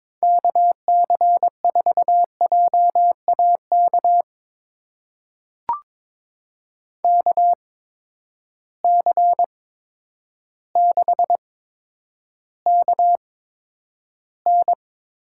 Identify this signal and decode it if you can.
Morse code, heard as a keyed tone, about 22 wpm, spaced as for 10 wpm, 700 Hz. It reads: KC4JAK KC6KN
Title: USA Call Signs - 6 Characters - 8x Character Spacing - 22wpm